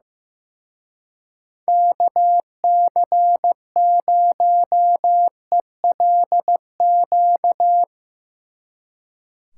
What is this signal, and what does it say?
Morse code, heard as a keyed tone, about 17 wpm, 700 Hz. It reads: KC0ELQ
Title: Speed Racing - USA 2x3 Call Signs 15wpm